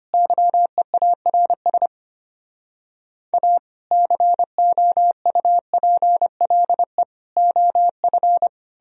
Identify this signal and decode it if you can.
Morse code, heard as a keyed tone, about 32 wpm, 700 Hz. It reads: YEARS A COUPLE OF